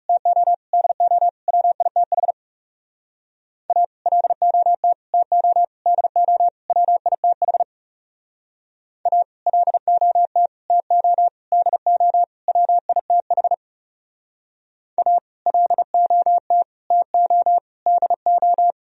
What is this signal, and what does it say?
Morse code, heard as a keyed tone, about 40 wpm, 700 Hz. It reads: TO DO WITH A LOT TO DO WITH A LOT TO DO WITH A LOT TO DO